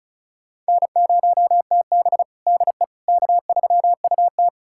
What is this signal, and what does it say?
Morse code, heard as a keyed tone, about 36 wpm, 700 Hz. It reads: N0TB DE K3UT
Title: USA Callsign DE Callsign - 4 Characters - 35wpm